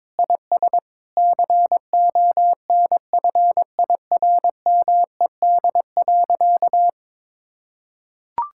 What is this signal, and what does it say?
Morse code, heard as a keyed tone, about 21 wpm, 700 Hz. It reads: IS CONFIRMED.
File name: practice / 22wpm / Sentences with 8-Word Phrases - 22wpm